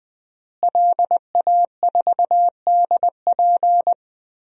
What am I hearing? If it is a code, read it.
Morse code, heard as a keyed tone, about 21 wpm, 700 Hz. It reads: LA4DP
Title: DXCC-100 Call Signs - 5 Characters - Repeated 2x - 20wpm